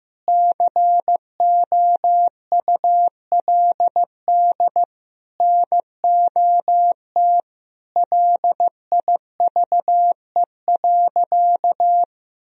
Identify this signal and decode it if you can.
Morse code, heard as a keyed tone, about 15 wpm, 700 Hz. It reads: COULD NOT LIVE.
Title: Sentences from Top 1000 Words - Review Part 4 - 15wpm